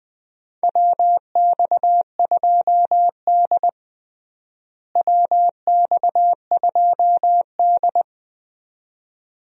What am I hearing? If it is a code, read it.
Morse code, heard as a keyed tone, about 20 wpm, 700 Hz. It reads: WX2D WX2D
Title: USA Call Signs - 4 Characters - Repeated 2x - 20wpm